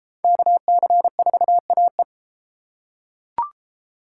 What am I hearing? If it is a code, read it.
Morse code, heard as a keyed tone, about 30 wpm, 700 Hz. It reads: KC4AE